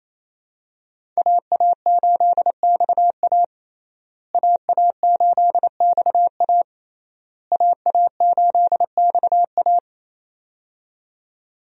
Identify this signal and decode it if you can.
Morse code, heard as a keyed tone, about 28 wpm, 700 Hz. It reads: AA8XA AA8XA AA8XA
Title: USA Call Signs - 5 Characters - Repeated 3x - 28wpm